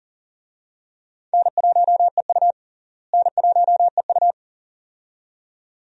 Morse code, heard as a keyed tone, about 38 wpm, 700 Hz. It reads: N1EU N1EU
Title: USA Call Signs - 4 Characters - Repeated 2x - 40wpm